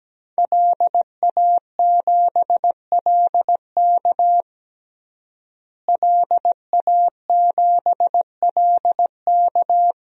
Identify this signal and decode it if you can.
Morse code, heard as a keyed tone, about 17 wpm, 700 Hz. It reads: LA7LK LA7LK
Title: DXCC-100 Call Signs - 5 Characters - Repeated 2x - 17wpm